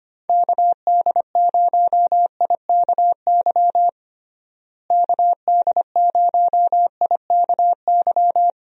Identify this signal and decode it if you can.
Morse code, heard as a keyed tone, about 25 wpm, 700 Hz. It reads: KD0IKY KD0IKY
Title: USA Call Signs - 6 Characters - Repeated 3x - 25wpm